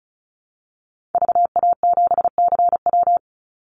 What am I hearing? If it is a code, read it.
Morse code, heard as a keyed tone, about 34 wpm, 700 Hz. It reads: VA7CW
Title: Call Signs Repeated 2x - 35wpm